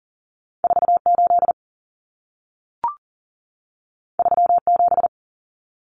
Morse code, heard as a keyed tone, about 40 wpm, 700 Hz. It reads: 48 37
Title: Numbers - 2-Digit 40wpm